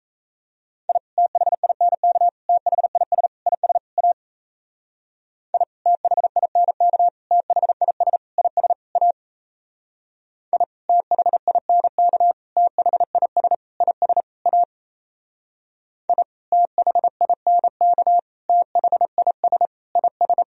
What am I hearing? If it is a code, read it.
Morse code, heard as a keyed tone, about 36 wpm, 700 Hz. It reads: I THINK THIS IS A I THINK THIS IS A I THINK THIS IS A I THINK THIS IS